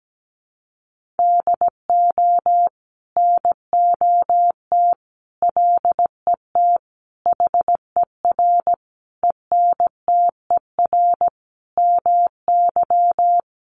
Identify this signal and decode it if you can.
Morse code, heard as a keyed tone, about 17 wpm, 700 Hz. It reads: DO NOT LET HER ENTER MY
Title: Sentences from Top 1300 Words 17wpm